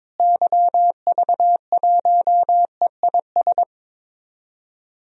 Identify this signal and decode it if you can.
Morse code, heard as a keyed tone, about 22 wpm, 700 Hz. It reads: YV1EIS
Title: DXCC-100 Call Signs - 6 Characters - Repeated 3x - 22wpm